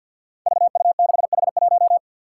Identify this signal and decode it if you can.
Morse code, heard as a keyed tone, about 49 wpm, 700 Hz. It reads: VU6H1
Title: Five Challenging Letter-Number Combinations 50wpm